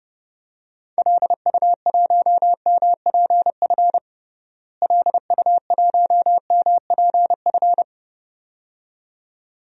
Morse code, heard as a keyed tone, about 32 wpm, 700 Hz. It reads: LU1MPF LU1MPF
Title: DXCC-100 Call Signs - 6 Characters - Repeated 2x - 30wpm